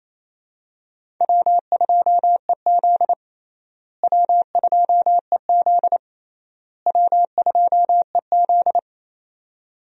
Morse code, heard as a keyed tone, about 28 wpm, 700 Hz. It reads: W2EZ W2EZ W2EZ
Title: USA Call Signs - 4 Characters - Repeated 3x - 28wpm